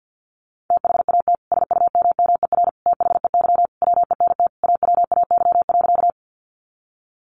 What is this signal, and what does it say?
Morse code, heard as a keyed tone, about 50 wpm, 700 Hz. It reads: THAT SUMMER THEY WENT AWAY.
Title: Sentences from Top 1100 Words 50wpm